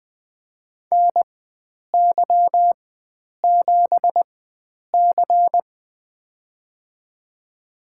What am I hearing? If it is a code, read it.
Morse code, heard as a keyed tone, about 20 wpm, spaced as for 9 wpm, 700 Hz. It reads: NY7C
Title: USA Call Signs - 4 Characters - 4x Character Spacing - 20wpm